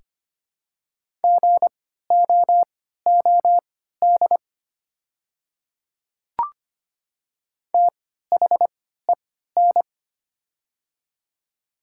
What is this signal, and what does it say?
Morse code, heard as a keyed tone, about 25 wpm, spaced as for 14 wpm, 700 Hz. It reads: GOOD THEN
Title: 3x Character Spacing - Top 200 Words - 25wpm